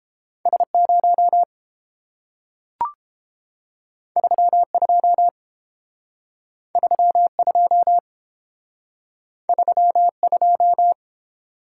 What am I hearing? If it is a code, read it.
Morse code, heard as a keyed tone, about 30 wpm, 700 Hz. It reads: S0 32 32 32